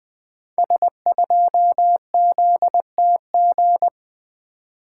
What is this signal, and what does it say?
Morse code, heard as a keyed tone, about 20 wpm, 700 Hz. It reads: S2ZTG
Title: Code Groups - 5 Letter:Numbers - 20wpm